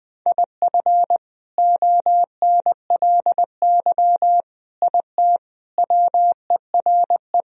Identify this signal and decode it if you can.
Morse code, heard as a keyed tone, about 21 wpm, 700 Hz. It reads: IF ONLY IT WERE
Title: Sentences from Top 500 Words and Abbreviations 20wpm